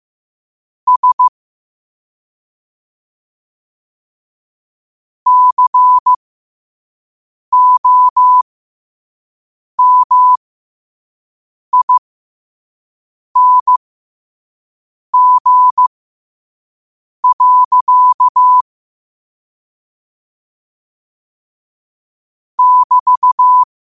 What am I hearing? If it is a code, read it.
Morse code, heard as a keyed tone, about 15 wpm, spaced as for 5 wpm, 1000 Hz. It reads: S COMING. <BT>
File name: qotdfiles / 5wpm